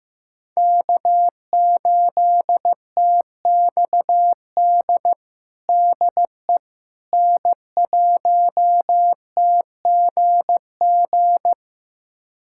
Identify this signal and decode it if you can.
Morse code, heard as a keyed tone, about 14 wpm, 700 Hz. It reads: K8TXD DE N1TGG
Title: USA Callsign DE Callsign - 5 Characters - 15wpm